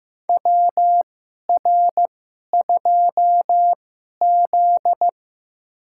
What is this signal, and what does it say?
Morse code, heard as a keyed tone, about 15 wpm, spaced as for 11 wpm, 700 Hz. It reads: WR2Z
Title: USA Call Signs - 2x Character Spacing - 15wpm